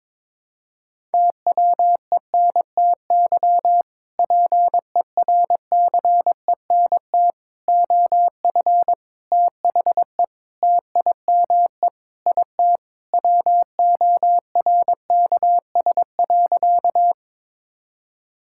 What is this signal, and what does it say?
Morse code, heard as a keyed tone, about 22 wpm, 700 Hz. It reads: TWENTY PERCENT OF THE TIME IT WORKS.
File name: practice / 22wpm / Sentences with 2-Word Phrases - 22wpm